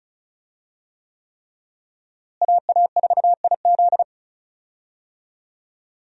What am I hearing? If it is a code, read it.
Morse code, heard as a keyed tone, about 36 wpm, 700 Hz. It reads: AA4IZ
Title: USA Call Signs - 5 Characters - Repeated 3x - 35wpm